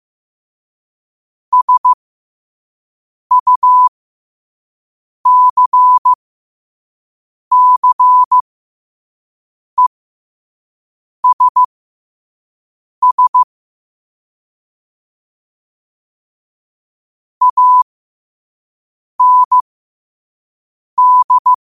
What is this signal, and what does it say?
Morse code, heard as a keyed tone, about 15 wpm, spaced as for 5 wpm, 1000 Hz. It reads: SUCCESS AND
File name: qotdfiles / 5wpm